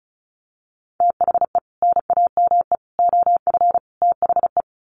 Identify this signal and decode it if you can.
Morse code, heard as a keyed tone, about 34 wpm, 700 Hz. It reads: THE NAME OF THE
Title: Sets of 4 Words 35wpm